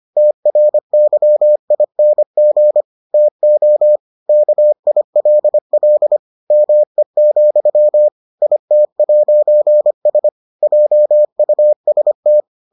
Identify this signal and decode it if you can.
Morse code, heard as a keyed tone, about 25 wpm, 600 Hz. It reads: TRYING TO KILL ME, IT'S JUST